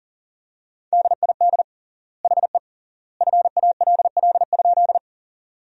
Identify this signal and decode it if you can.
Morse code, heard as a keyed tone, about 38 wpm, 700 Hz. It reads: DID HE FALL?
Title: 3x Word-Spacing - Sentences from Top 500 Words - 40wpm